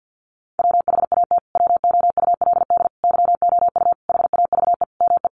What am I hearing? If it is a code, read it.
Morse code, heard as a keyed tone, about 50 wpm, 700 Hz. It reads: WHAT WOULD YOU HAVE ME
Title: Sentences from Top 200 Words 50wpm